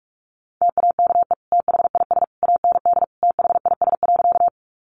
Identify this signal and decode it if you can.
Morse code, heard as a keyed tone, about 44 wpm, 700 Hz. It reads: TAKE THIS AND THIS.
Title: Sentences from Top 200 Words 45wpm